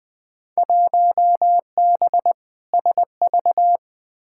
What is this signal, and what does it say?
Morse code, heard as a keyed tone, about 20 wpm, 700 Hz. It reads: 1B SV
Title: Field Day Exchanges Encore - 20wpm